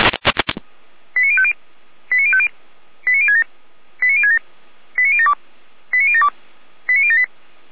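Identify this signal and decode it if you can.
Five-tone selective call (ZVEI), heard as quick runs of five tones, tones of about 70 ms. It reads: 89050 89050 89067 89067 89072 89072 89078
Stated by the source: ZVEI five-tone call received by SDR, fire brigade feuerwehr-orlamuende.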